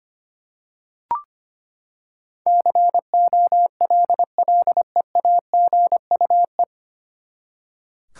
Morse code, heard as a keyed tone, about 25 wpm, 700 Hz. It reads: COLLEAGUE
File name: Nine Letter Words - 25wpm